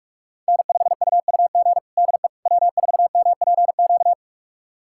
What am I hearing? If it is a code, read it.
Morse code, heard as a keyed tone, about 46 wpm, 700 Hz. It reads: N5UUG DE W4MPQ
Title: USA Callsign DE Callsign - 5 Characters - 45wpm